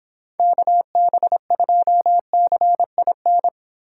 Morse code, heard as a keyed tone, about 29 wpm, 700 Hz. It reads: KB2CIN